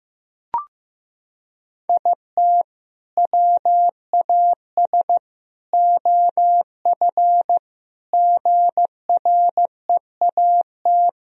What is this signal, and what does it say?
Morse code, heard as a keyed tone, about 15 wpm, 700 Hz. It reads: IT WAS OF GREAT
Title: Sentences from Top 1000 Words - Review Part 2 - 15wpm